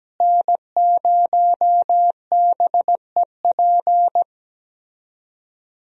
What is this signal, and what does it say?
Morse code, heard as a keyed tone, about 17 wpm, 700 Hz. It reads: N0BEP
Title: USA Call Signs - 5 Characters - Repeated 3x - 17wpm